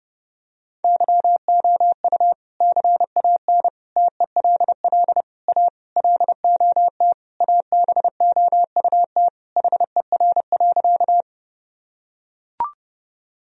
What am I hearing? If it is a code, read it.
Morse code, heard as a keyed tone, about 32 wpm, 700 Hz. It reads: YOU CAN TELL A LOT ABOUT HER.
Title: Sentences from Top 1000 Words - Review Part 3 - 30wpm